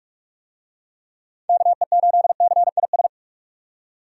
Morse code, heard as a keyed tone, about 45 wpm, 700 Hz. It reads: KE8CIS
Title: USA Call Signs - 6 Characters - 45wpm